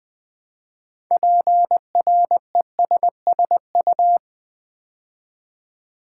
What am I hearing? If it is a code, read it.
Morse code, heard as a keyed tone, about 20 wpm, 700 Hz. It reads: PRESSU
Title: Build A Word - 2 Letters At A Time - 8-Letter Words - 20wpm